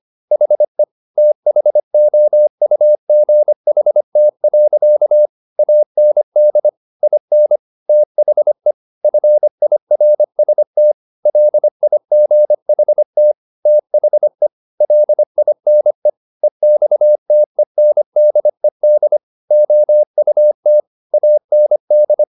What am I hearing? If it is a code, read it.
Morse code, heard as a keyed tone, about 25 wpm, 600 Hz. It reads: HE THOUGHT. AND IN THE FIRST LIGHT THE LINE EXTENDED OUT AND
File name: the old man and the sea part 1 25wpm